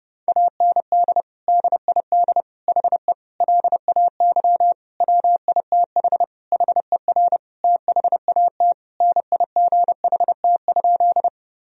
Morse code, heard as a keyed tone, about 30 wpm, 700 Hz. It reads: AND DID HE LAY WITH HER THAT NIGHT?
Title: Sentences from Top 1000 Words - Review Part 2 - 30wpm